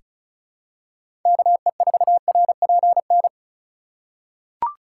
Morse code, heard as a keyed tone, about 34 wpm, 700 Hz. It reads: KE4RPN
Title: USA Call Signs - 6 Characters - 3x Character Spacing - 35wpm